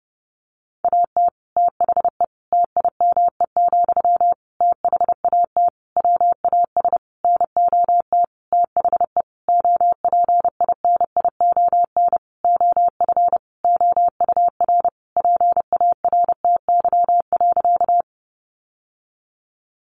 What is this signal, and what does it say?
Morse code, heard as a keyed tone, about 28 wpm, 700 Hz. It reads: AT THE TIME, THAT WAS NOT THE OPINION OF OUR PARTY.
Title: Sentences from Top 1400 Words 30wpm